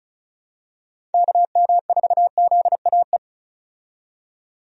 Morse code, heard as a keyed tone, about 36 wpm, 700 Hz. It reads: KM4ZAE
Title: USA Call Signs - 6 Characters - 35wpm